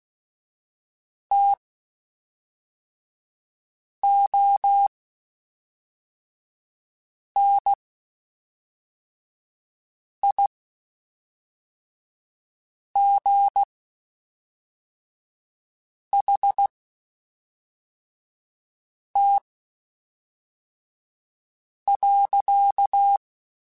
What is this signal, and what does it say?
Morse code, heard as a keyed tone, about 16 wpm, spaced as for 3 wpm, 800 Hz. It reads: TONIGHT.
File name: (spaced) / QSOs Spaced-mp3s / WA7QOX 1 @5 sp